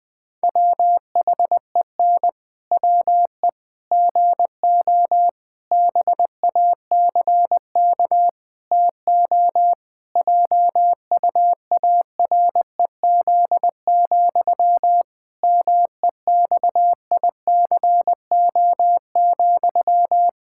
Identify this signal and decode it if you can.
Morse code, heard as a keyed tone, about 20 wpm, 700 Hz. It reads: WHEN WE GO BACK TO JUAREZ, MEXICO,
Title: Pangram Challenge v2 - 20wpm